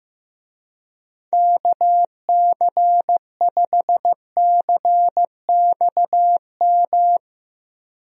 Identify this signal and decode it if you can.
Morse code, heard as a keyed tone, about 15 wpm, 700 Hz. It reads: KC5CXM
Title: USA Call Signs - 6 Characters - Repeated 3x - 15wpm